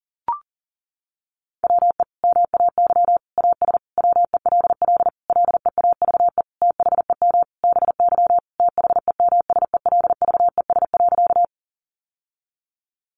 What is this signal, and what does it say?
Morse code, heard as a keyed tone, about 40 wpm, 700 Hz. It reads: WE MAY AS WELL LEAVE THEM BY THEMSELVES.
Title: Sentences from Top 800 Words 40wpm